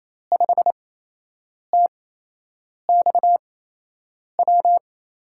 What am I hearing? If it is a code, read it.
Morse code, heard as a keyed tone, about 28 wpm, spaced as for 8 wpm, 700 Hz. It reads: 5TXW
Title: USA Call Signs - 8x Character Spacing - 28wpm